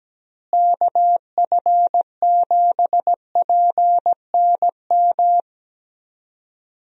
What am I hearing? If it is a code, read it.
Morse code, heard as a keyed tone, about 17 wpm, 700 Hz. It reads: KF7PNM
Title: USA Call Signs - 6 Characters - Repeated 2x - 17wpm